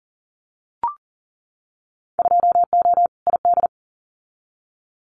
Morse code, heard as a keyed tone, about 40 wpm, 700 Hz. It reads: 2O ID
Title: Winter Field Day Exchanges 40wpm